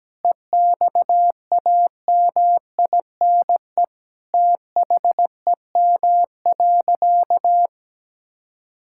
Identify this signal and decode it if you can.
Morse code, heard as a keyed tone, about 17 wpm, 700 Hz. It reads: EXAMINE THEM.
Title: Sentences from Top 1600 Words 17wpm